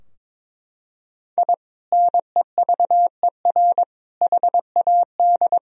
Morse code, heard as a keyed tone, about 22 wpm, 700 Hz. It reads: I NEVER HAD
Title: Sentences from Top 1000 Words - Review Part 4 - 22wpm